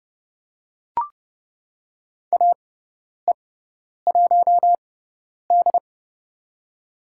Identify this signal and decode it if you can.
Morse code, heard as a keyed tone, about 29 wpm, spaced as for 10 wpm, 700 Hz. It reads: AE1D
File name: Lesson 17g - Letter u - Call Signs 30wpm